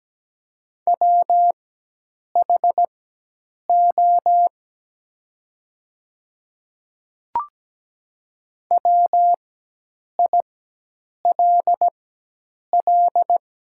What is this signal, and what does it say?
Morse code, heard as a keyed tone, about 17 wpm, spaced as for 8 wpm, 700 Hz. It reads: WHO WILL
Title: 4x Character Spacing - Top 100 Words - 17wpm